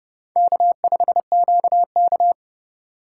Morse code, heard as a keyed tone, about 31 wpm, 700 Hz. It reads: K5QK
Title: USA Call Signs - 2x Character Spacing - 30wpm